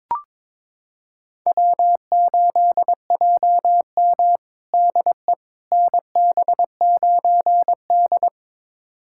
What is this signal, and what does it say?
Morse code, heard as a keyed tone, about 22 wpm, 700 Hz. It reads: W8JM DE NB9D
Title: USA Callsign DE Callsign - 4 Characters - 22wpm